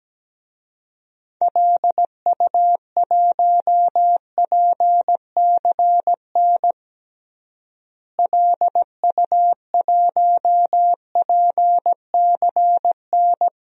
Morse code, heard as a keyed tone, about 17 wpm, 700 Hz. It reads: LU1PCN LU1PCN
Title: DXCC-100 Call Signs - 6 Characters - Repeated 3x - 17wpm